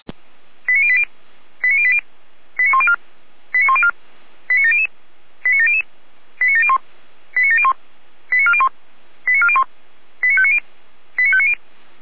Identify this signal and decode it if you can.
Five-tone selective call (ZVEI), heard as quick runs of five tones, tones of about 70 ms. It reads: 89080 89080 89104 89104 89700 89700 89701 89701 89401 89401 89590 89590